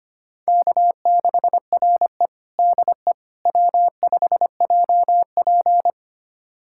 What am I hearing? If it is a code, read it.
Morse code, heard as a keyed tone, about 25 wpm, 700 Hz. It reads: K6RE DE W5JP
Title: USA Callsign DE Callsign - 4 Characters - 25wpm